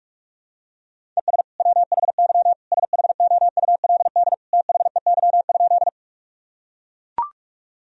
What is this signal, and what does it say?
Morse code, heard as a keyed tone, about 43 wpm, 700 Hz. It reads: ES WHY SHOULD THEY?
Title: Sentences from Top 300 Words and Abbreviations 45wpm